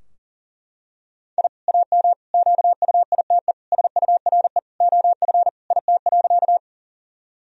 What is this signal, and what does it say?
Morse code, heard as a keyed tone, about 37 wpm, 700 Hz. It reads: I AM QUITE SURE OF IT.
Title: Sentences from Top 1000 Words - Review Part 2 - 40wpm